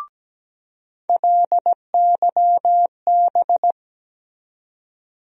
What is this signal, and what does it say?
Morse code, heard as a keyed tone, about 17 wpm, 700 Hz. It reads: LYB
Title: USA Call Sign Postfixes - 3 Letters 17wpm